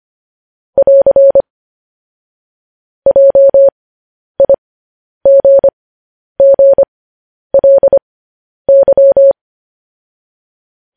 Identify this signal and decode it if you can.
Morse code, heard as a keyed tone, about 25 wpm, spaced as for 10 wpm, 550 Hz. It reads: <AR> JIGGLY
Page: news headlines in morse code at 10 wpm